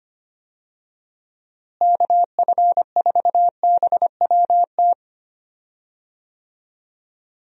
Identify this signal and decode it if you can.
Morse code, heard as a keyed tone, about 25 wpm, 700 Hz. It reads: KF4BWT
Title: USA Call Signs - 6 Characters - Repeated 3x - 25wpm